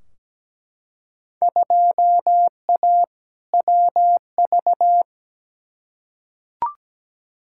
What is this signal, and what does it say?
Morse code, heard as a keyed tone, about 17 wpm, 700 Hz. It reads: 2A WV
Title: Field Day Exchanges Encore - 17wpm